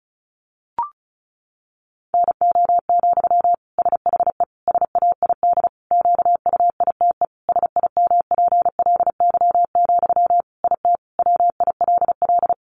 Morse code, heard as a keyed tone, about 34 wpm, 700 Hz. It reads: NO, SHE SAID QUITE SIMPLY, IT WILL
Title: Sentences from Top 800 Words 35wpm